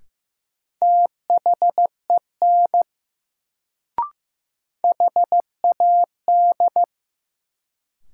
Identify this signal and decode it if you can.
Morse code, heard as a keyed tone, about 15 wpm, 700 Hz. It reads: THEN HAD
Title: Sentences from Top 100 Words - New Word Drill - 15wpm